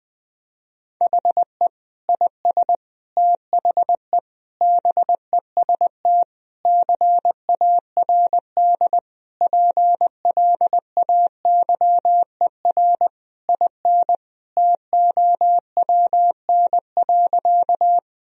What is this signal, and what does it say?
Morse code, heard as a keyed tone, about 21 wpm, 700 Hz. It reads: HE IS THE BEST CARD PLAYER IN TOWN.
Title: Sentences from Top 1000 Words - Review Part 4 - 20wpm